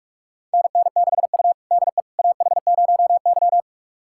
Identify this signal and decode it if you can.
Morse code, heard as a keyed tone, about 44 wpm, 690 Hz. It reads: NN6U DE AH0Y
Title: USA Callsign DE Callsign - 4 Characters - 45wpm